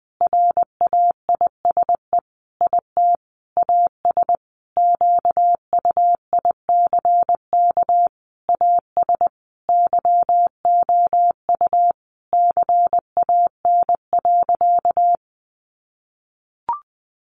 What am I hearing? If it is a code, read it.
Morse code, heard as a keyed tone, about 20 wpm, 700 Hz. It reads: RAISE IT AS QUICK AS YOU CAN.
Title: Sentences from Top 1400 Words 20wpm